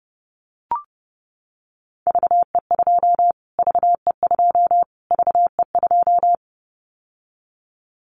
Morse code, heard as a keyed tone, about 30 wpm, 700 Hz. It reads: VE2 VE2 VE2
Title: Callsign Prefixes for Top 25 DXCC - Repeated 3x - 30wpm